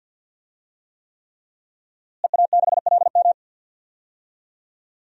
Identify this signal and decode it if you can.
Morse code, heard as a keyed tone, about 51 wpm, 700 Hz. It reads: EA6LM
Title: DXCC-100 Call Signs - 5 Characters - Repeated 3x - 50wpm